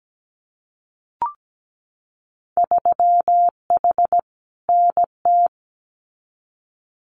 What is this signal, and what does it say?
Morse code, heard as a keyed tone, about 17 wpm, 700 Hz. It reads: 3H NT
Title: Winter Field Day Exchanges Encore 17wpm